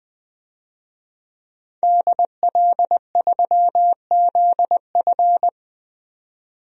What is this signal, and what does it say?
Morse code, heard as a keyed tone, about 20 wpm, 700 Hz. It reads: DL3ZF
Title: DXCC-100 Call Signs - 5 Characters - 20wpm